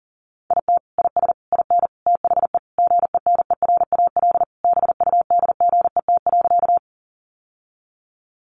Morse code, heard as a keyed tone, about 37 wpm, 700 Hz. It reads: IT IS IN THE GENERAL BUDGET.
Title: Sentences from Top 1200 Words 40wpm